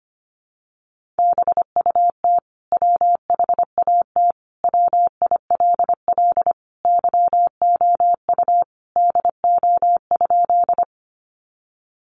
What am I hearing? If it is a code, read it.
Morse code, heard as a keyed tone, about 25 wpm, 700 Hz. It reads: BUT WHAT WILL YOU DO?
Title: Sentences from Top 100 Words 25wpm